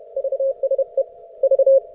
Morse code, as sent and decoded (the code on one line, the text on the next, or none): ...- ... / ...-
VSE V